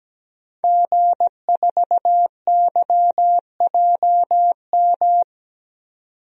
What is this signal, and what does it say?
Morse code, heard as a keyed tone, about 17 wpm, 700 Hz. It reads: G4YJM
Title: DXCC-100 Call Signs - 5 Characters - 17wpm